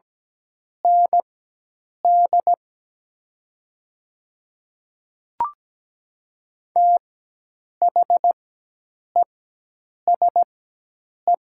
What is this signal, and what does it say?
Morse code, heard as a keyed tone, about 17 wpm, spaced as for 8 wpm, 700 Hz. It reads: ND THESE
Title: 4x Character Spacing - Top 500 Words - 17wpm